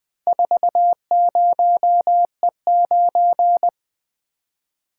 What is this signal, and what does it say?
Morse code, heard as a keyed tone, about 20 wpm, 700 Hz. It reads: 40E9
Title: Four Letter-Number Combinations - 20wpm